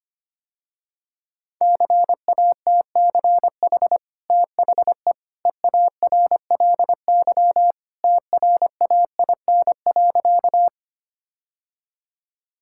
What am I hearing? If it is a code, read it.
Morse code, heard as a keyed tone, about 25 wpm, 700 Hz. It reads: CATCH THE EARLY TRAIN.
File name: Easy Sentences with Hints - 25wpm